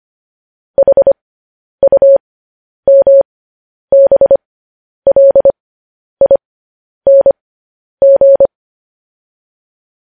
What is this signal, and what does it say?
Morse code, heard as a keyed tone, about 25 wpm, spaced as for 10 wpm, 550 Hz. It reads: HUMBLING
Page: news-headlines-in-morse-code-at-10-wpm